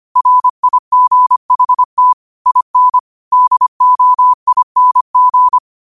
Morse code, heard as a keyed tone, about 25 wpm, 1000 Hz. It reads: RIGHT IN DOING
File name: qotdfiles / 25wpm